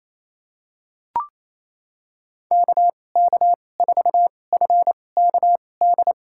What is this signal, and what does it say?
Morse code, heard as a keyed tone, about 28 wpm, spaced as for 20 wpm, 700 Hz. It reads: KK4FKD
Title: USA Call Signs - 2x Character Spacing - 28wpm